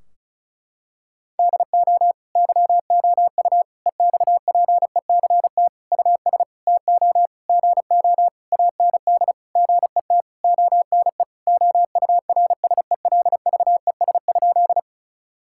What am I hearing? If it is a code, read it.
Morse code, heard as a keyed tone, about 34 wpm, 700 Hz. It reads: DO YOU EXPECT US TO GO AND GET ONE OURSELVES?
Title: Sentences from Top 1600 Words 35wpm